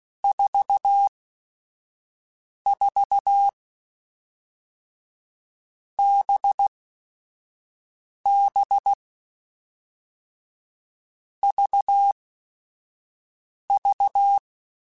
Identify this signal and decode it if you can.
Morse code, heard as a keyed tone, about 16 wpm, spaced as for 5 wpm, 800 Hz. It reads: 44 BB VV
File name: Groups@5 / Groups@5 mp3s / VB46@5